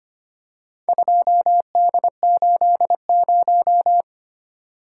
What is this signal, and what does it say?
Morse code, heard as a keyed tone, about 25 wpm, 690 Hz. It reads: 2D80
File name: practice / 25wpm / Four Letter-Number Combinations - 25wpm